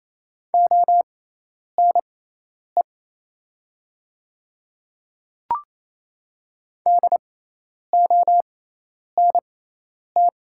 Morse code, heard as a keyed tone, about 28 wpm, spaced as for 10 wpm, 700 Hz. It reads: ONE DONT
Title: 6x Character Spacing - Top 300 Words - 28wpm